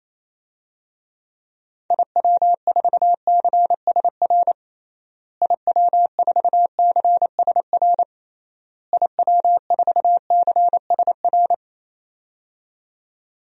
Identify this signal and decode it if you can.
Morse code, heard as a keyed tone, about 28 wpm, 700 Hz. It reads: IW4CSR IW4CSR IW4CSR
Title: DXCC-100 Call Signs - 6 Characters - Repeated 3x - 28wpm